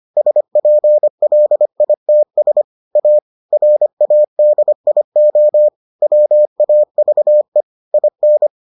Morse code, heard as a keyed tone, about 25 wpm, 600 Hz. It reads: SPLITS A RADIO WAVE IN